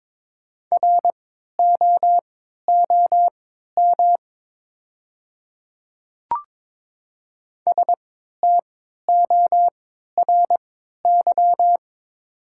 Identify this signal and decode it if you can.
Morse code, heard as a keyed tone, about 22 wpm, spaced as for 12 wpm, 700 Hz. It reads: ROOM STORY
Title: 3x Character Spacing - Top 500 Words - 22wpm